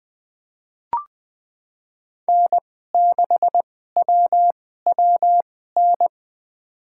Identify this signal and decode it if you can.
Morse code, heard as a keyed tone, about 21 wpm, spaced as for 15 wpm, 700 Hz. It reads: N6WWN
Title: USA Call Signs - 5 Characters - 2x Character Spacing - 20wpm